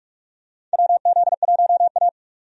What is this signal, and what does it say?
Morse code, heard as a keyed tone, about 43 wpm, 700 Hz. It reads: WZ1A
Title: USA Call Signs - 4 Characters - 45wpm